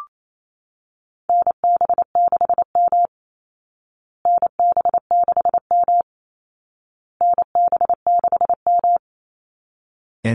Morse code, heard as a keyed tone, about 28 wpm, 700 Hz. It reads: NB6M NB6M NB6M
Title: Call Signs Repeated 2x - 28wpm